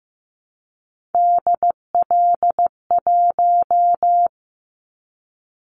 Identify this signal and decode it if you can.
Morse code, heard as a keyed tone, about 15 wpm, 700 Hz. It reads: DL1
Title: Callsign Prefixes for Top 25 DXCC - 15wpm